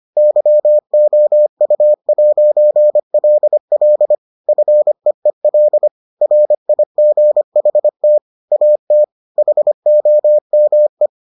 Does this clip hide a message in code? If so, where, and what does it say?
Yes: Morse code from 0.2 to 11.1 s, heard as a keyed tone, about 25 wpm, 600 Hz. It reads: YOU'LL FEEL RIGHT AT HOME